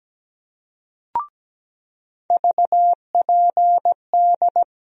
Morse code, heard as a keyed tone, about 17 wpm, 700 Hz. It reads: VPD
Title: USA Call Sign Postfixes - 3 Letters 17wpm